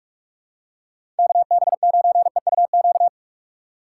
Morse code, heard as a keyed tone, about 45 wpm, 700 Hz. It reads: KB9EUQ
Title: USA Call Signs - 6 Characters - 2x Character Spacing - 45wpm